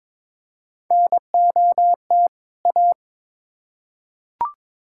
Morse code, heard as a keyed tone, about 22 wpm, 700 Hz. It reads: NOT A
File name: Build A Phrase - 2-Word Phrases - 22wpm